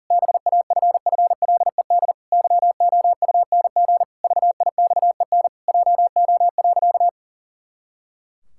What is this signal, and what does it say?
Morse code, heard as a keyed tone, about 40 wpm, 700 Hz. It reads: BAFFLED YOUNG VIXEN JO.
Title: Pangram Challenge v2 - 40wpm